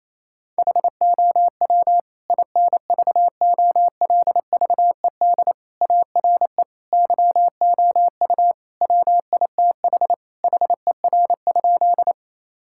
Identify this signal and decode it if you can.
Morse code, heard as a keyed tone, about 28 wpm, 700 Hz. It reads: HOW INVOLVED ARE YOU WITH HER?
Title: Sentences from Top 1000 Words - Review Part 5 - 28wpm